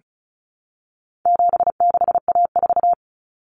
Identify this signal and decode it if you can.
Morse code, heard as a keyed tone, about 34 wpm, 700 Hz. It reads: 76A4
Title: Common Cut Numbers - 4-Digit 35wpm